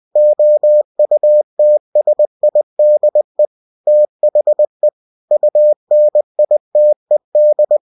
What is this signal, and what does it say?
Morse code, heard as a keyed tone, about 20 wpm, 600 Hz. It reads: OUTSIDE THE UNITED